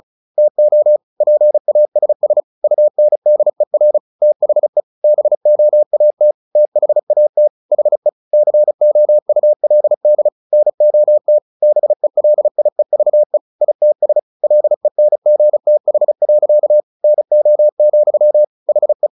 Morse code, heard as a keyed tone, about 34 wpm, 600 Hz. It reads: TO PASS UNDER THE BOAT THAT HE COULD NOT BELIEVE ITS LENGTH. NO, HE
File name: the old man and the sea 2 35wpm